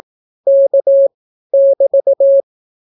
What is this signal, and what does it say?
Morse code, heard as a keyed tone, about 18 wpm, 550 Hz. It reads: K <BT>
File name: naukapodstaw / skroty / skroty